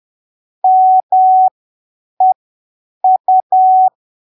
Morse code, heard as a keyed tone, about 10 wpm, spaced as for 7 wpm, 750 Hz. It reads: MEU
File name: arquivos / texto / texto 26 10ppm